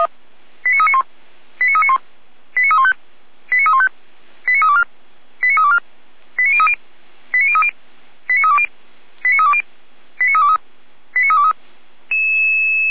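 Five-tone selective call (ZVEI), heard as quick runs of five tones, tones of about 70 ms. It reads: 89391 89391 89315 89315 89324 89324 89030 89030 89320 89320 89323 89323
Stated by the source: Five-tone ZVEI selective call received by SDR.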